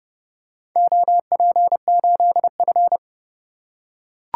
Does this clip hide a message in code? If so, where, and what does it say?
Yes: Morse code from 0.8 to 3.0 s, heard as a keyed tone, about 31 wpm, 700 Hz. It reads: OP8F